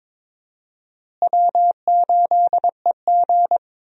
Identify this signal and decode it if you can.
Morse code, heard as a keyed tone, about 21 wpm, 700 Hz. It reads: W8EG
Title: USA Call Signs - 4 Characters - Repeated 3x - 22wpm